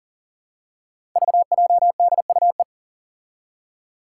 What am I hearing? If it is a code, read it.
Morse code, heard as a keyed tone, about 40 wpm, 690 Hz. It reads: VJDUE
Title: Five Challenging Letter-Number Combinations 40wpm